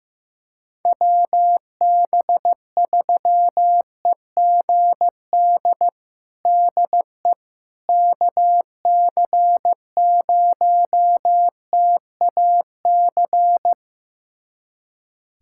Morse code, heard as a keyed tone, about 15 wpm, 700 Hz. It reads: WB3EGD DE KC0TAC
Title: USA Callsign DE Callsign - 6 Characters - 15wpm